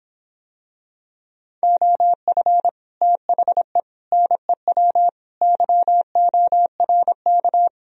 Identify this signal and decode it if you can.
Morse code, heard as a keyed tone, about 26 wpm, 700 Hz. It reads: OF THE NEW YORK